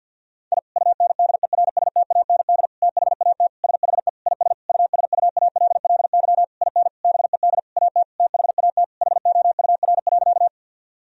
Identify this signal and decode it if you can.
Morse code, heard as a keyed tone, about 52 wpm, 700 Hz. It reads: I UNDERSTAND THAT SHE IS USUALLY IN BED AT THAT HOUR.
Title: Sentences from Top 1000 Words - Review Part 3 - 50wpm